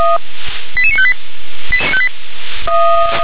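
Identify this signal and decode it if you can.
Five-tone selective call (ZVEI), heard as quick runs of five tones, tones of about 70 ms. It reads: 88958
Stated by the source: ZVEI five-tone selective call received with an SDR.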